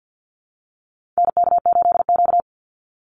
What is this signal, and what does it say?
Morse code, heard as a keyed tone, about 48 wpm, 700 Hz. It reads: NK8Q
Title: NA SOTA Activator Callsigns 50wpm